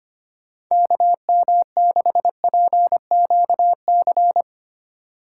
Morse code, heard as a keyed tone, about 25 wpm, 700 Hz. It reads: KM6PQC
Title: USA Call Signs - 6 Characters - 4x Character Spacing - 25wpm